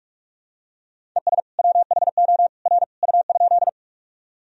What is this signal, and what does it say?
Morse code, heard as a keyed tone, about 46 wpm, 700 Hz. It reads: ES WHO R U?
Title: Sentences from Top 100 Words and Abbreviations 45wpm